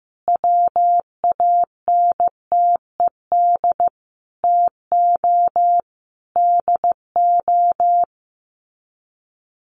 Morse code, heard as a keyed tone, about 15 wpm, 700 Hz. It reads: WANTED TO DO
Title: Sets of 5 Words Encore2 15wpm